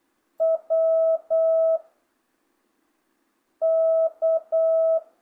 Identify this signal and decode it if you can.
Morse code, heard as a keyed tone, about 8 wpm, 660 Hz. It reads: W K
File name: jh8cbh / 2025cw